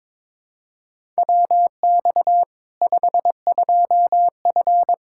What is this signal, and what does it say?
Morse code, heard as a keyed tone, about 21 wpm, 700 Hz. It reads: WX 52F
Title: K1USN SST QSO Practice 22wpm